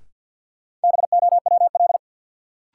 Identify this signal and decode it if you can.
Morse code, heard as a keyed tone, about 51 wpm, 700 Hz. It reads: BOWL
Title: Sentences from Top 1200 Words - New Word Drill Alt - 50wpm